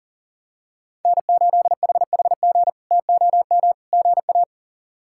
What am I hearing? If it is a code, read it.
Morse code, heard as a keyed tone, about 40 wpm, 700 Hz. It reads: N8HHG TOM GA
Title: K1USN SST QSO Practice - 40wpm